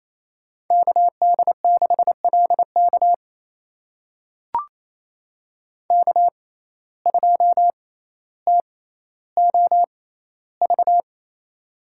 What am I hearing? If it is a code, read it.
Morse code, heard as a keyed tone, about 28 wpm, spaced as for 14 wpm, 700 Hz. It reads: KD6LK K2TOV
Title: USA Call Signs - 5 Characters - 6x Character Spacing - 28wpm